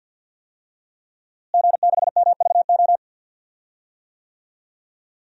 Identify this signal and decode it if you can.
Morse code, heard as a keyed tone, about 52 wpm, 700 Hz. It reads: G6MVO
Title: DXCC-100 Call Signs - 5 Characters - Repeated 2x - 50wpm